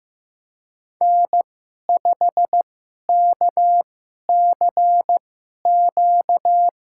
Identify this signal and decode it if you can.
Morse code, heard as a keyed tone, about 15 wpm, spaced as for 11 wpm, 700 Hz. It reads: N5KCQ
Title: USA Call Signs - 5 Characters - 2x Character Spacing - 15wpm